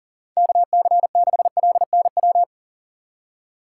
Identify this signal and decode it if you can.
Morse code, heard as a keyed tone, about 40 wpm, 700 Hz. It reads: KC6LNW
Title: USA Call Signs - 6 Characters - 4x Character Spacing - 40wpm